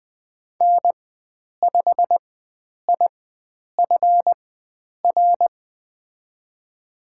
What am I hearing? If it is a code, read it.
Morse code, heard as a keyed tone, about 20 wpm, spaced as for 9 wpm, 700 Hz. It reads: N5IFR
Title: USA Call Signs - 5 Characters - 4x Character Spacing - 20wpm